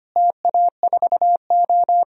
Morse code, heard as a keyed tone, about 25 wpm, 700 Hz. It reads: TA4O
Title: DXCC-100 Call Signs - 4 Characters - Repeated 3x - 25wpm